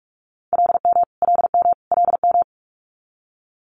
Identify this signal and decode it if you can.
Morse code, heard as a keyed tone, about 46 wpm, 700 Hz. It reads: LM LM LM
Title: Multi-letter Phonemes Repeated 3x 45wpm